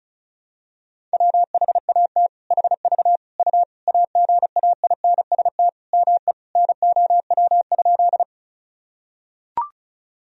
Morse code, heard as a keyed tone, about 36 wpm, 700 Hz. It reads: WHAT HV U AGAINST ME NOW?
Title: Sentences from Top 400 Words and Abbreviations 35wpm